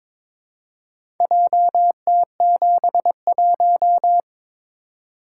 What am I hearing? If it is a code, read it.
Morse code, heard as a keyed tone, about 22 wpm, 700 Hz. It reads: JT71
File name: practice / 22wpm / Four Letter-Number Combinations - 22wpm